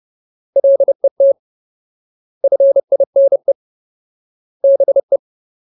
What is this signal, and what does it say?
Morse code, heard as a keyed tone, about 30 wpm, 550 Hz. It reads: LET FINE BE
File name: Ultimate Word List / 30WPM Ultimate